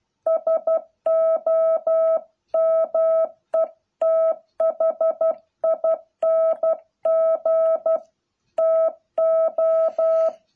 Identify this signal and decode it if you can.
Morse code, heard as a keyed tone, about 12 wpm, 660 Hz. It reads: SOMETHING TO